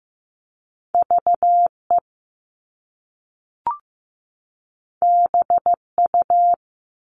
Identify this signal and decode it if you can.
Morse code, heard as a keyed tone, about 15 wpm, 700 Hz. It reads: VE BU
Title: All Cut Numbers - 2-Digit 15wpm